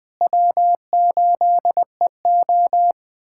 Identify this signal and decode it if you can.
Morse code, heard as a keyed tone, about 20 wpm, 700 Hz. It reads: W8EO
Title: USA Callsign DE Callsign - 4 Characters - 20wpm